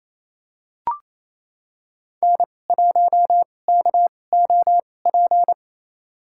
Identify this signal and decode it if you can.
Morse code, heard as a keyed tone, about 28 wpm, spaced as for 20 wpm, 700 Hz. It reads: N1KOP
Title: USA Call Signs - 2x Character Spacing - 28wpm